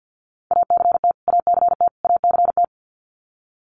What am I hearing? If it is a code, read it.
Morse code, heard as a keyed tone, about 49 wpm, 700 Hz. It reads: ACT ACT ACT
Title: Top 200 Words Repeated 50wpm